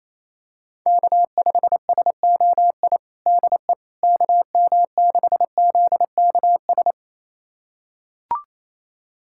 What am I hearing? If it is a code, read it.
Morse code, heard as a keyed tone, about 28 wpm, 700 Hz. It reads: K5SOI DE KM6ZKS
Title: USA Callsign DE Callsign 28wpm